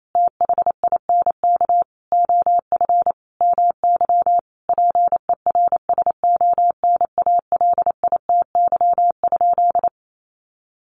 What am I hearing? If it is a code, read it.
Morse code, heard as a keyed tone, about 28 wpm, 700 Hz. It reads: THINK OF MY PERSONALITY?
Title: Sentences from Top 1400 Words 28wpm